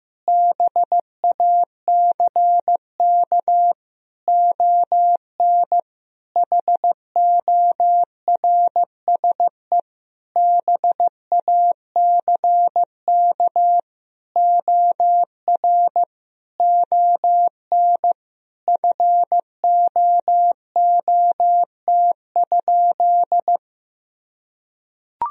Morse code, heard as a keyed tone, about 16 wpm, 700 Hz. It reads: BACK ON HORSE BACK OR ON FOOT?
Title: Sentences from Top 1000 Words - Review Part 3 - 15wpm